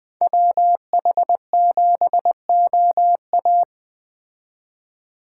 Morse code, heard as a keyed tone, about 20 wpm, 700 Hz. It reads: WH7OA
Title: Speed Racing - USA 2x2 Call Signs 17wpm